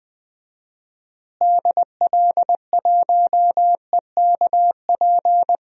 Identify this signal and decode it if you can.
Morse code, heard as a keyed tone, about 20 wpm, 700 Hz. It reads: DL1EKP
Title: DXCC-100 Call Signs - 6 Characters - Repeated 2x - 20wpm